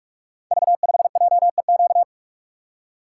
Morse code, heard as a keyed tone, about 44 wpm, 700 Hz. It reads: V5JEQ
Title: Code Groups - 5 Letter:Numbers - 45wpm